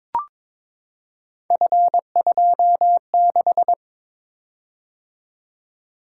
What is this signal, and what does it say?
Morse code, heard as a keyed tone, about 21 wpm, 700 Hz. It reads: F26
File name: practice / 22wpm / Three Letter-Number Combinations - 22wpm